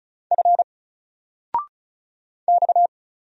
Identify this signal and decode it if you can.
Morse code, heard as a keyed tone, about 36 wpm, 700 Hz. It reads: F X
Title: Single Character (Letter:Number:Punctuation:Prosign) 35wpm